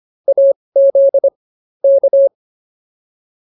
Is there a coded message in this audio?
Yes: Morse code from 0.3 to 2.3 s, heard as a keyed tone, about 24 wpm, spaced as for 19 wpm, 550 Hz. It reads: AZ K